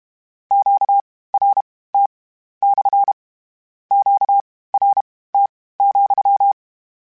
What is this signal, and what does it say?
Morse code, heard as a keyed tone, about 32 wpm, spaced as for 18 wpm, 800 Hz. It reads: QRT / QRT,
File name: Q Calls / Q Calls-mp3s / Q CALLS SPSL @30